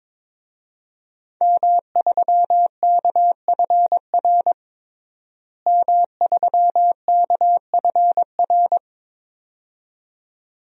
Morse code, heard as a keyed tone, about 22 wpm, 700 Hz. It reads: M3KFR M3KFR
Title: DXCC-100 Call Signs - 5 Characters - Repeated 2x - 22wpm